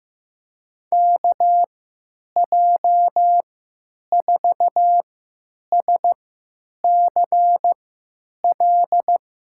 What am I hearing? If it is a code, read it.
Morse code, heard as a keyed tone, about 15 wpm, spaced as for 9 wpm, 700 Hz. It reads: KJ4SCL
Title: USA Call Signs - 6 Characters - 3x Character Spacing - 15wpm